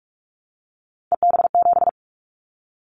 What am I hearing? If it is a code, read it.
Morse code, heard as a keyed tone, about 44 wpm, 700 Hz. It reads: EB7
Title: Callsign Prefixes for Top 25 DXCC - 45wpm